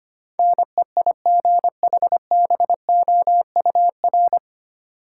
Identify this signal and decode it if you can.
Morse code, heard as a keyed tone, about 25 wpm, 700 Hz. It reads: NEIGHBOUR
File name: practice / 25wpm / 9 Letter Words - First 3 Letters Given - 25wpm